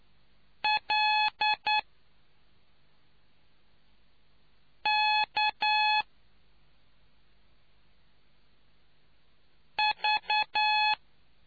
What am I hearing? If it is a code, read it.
Morse code, heard as a keyed tone, about 9 wpm, spaced as for 3 wpm, 850 Hz. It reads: LK V